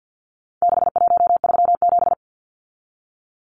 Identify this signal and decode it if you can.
Morse code, heard as a keyed tone, about 50 wpm, 700 Hz. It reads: 6137
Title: Numbers - 4-Digit 50wpm